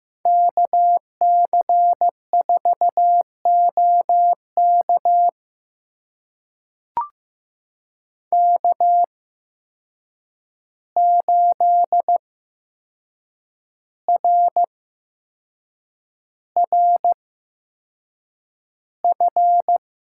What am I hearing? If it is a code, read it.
Morse code, heard as a keyed tone, about 15 wpm, spaced as for 6 wpm, 700 Hz. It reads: KC4OK K8RRF
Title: USA Call Signs - 5 Characters - 8x Character Spacing - 15wpm